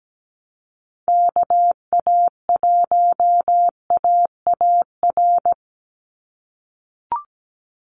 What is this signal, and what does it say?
Morse code, heard as a keyed tone, about 17 wpm, 700 Hz. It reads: KA1AAR
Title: USA Call Signs 17wpm